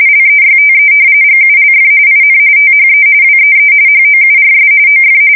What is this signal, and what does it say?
RTTY, about 45.45 baud. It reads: LEYTE GULF IN THE WEE HOURS OF
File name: fsk170 / battleshipnj